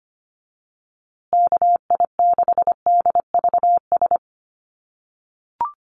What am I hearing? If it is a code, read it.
Morse code, heard as a keyed tone, about 25 wpm, 700 Hz. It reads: KI6DVS
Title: USA Call Signs Repeated 2x 25wpm